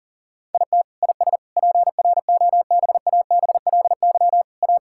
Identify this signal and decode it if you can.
Morse code, heard as a keyed tone, about 39 wpm, 700 Hz. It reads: IT IS PROBABLY A
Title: Sentences from Top 1000 Words - Review Part 6 - 40wpm